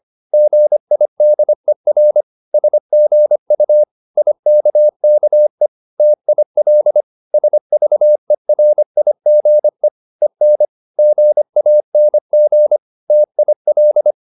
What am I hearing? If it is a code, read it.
Morse code, heard as a keyed tone, about 25 wpm, 600 Hz. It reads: GIDER SGU IKKE TIL SVERIGE EN GANG TIL
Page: Ferie 25 wpm